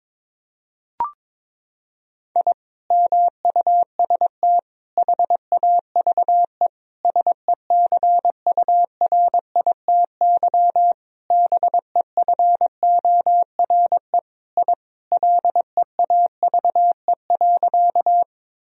Morse code, heard as a keyed tone, about 22 wpm, 700 Hz. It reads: I MUST HAVE SECURITY BEFORE I LEAVE.
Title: Sentences from Top 1000 Words - Review Part 4 - 22wpm